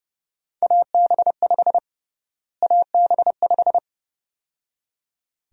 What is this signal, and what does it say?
Morse code, heard as a keyed tone, about 30 wpm, 700 Hz. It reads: AB5 AB5
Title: USA Call Sign Prefixes - Repeated 2x - 30wpm